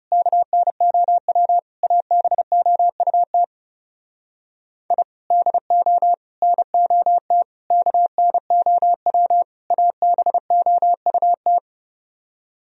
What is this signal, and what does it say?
Morse code, heard as a keyed tone, about 31 wpm, 700 Hz. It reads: KNOW ABOUT I DO NOT KNOW ABOUT
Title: Speed-Racing - Sets of 5 Words 30wpm